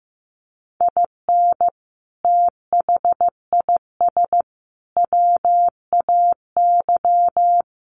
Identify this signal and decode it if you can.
Morse code, heard as a keyed tone, about 15 wpm, 700 Hz. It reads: IN THIS WAY
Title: Sentences from Top 1200 Words 15wpm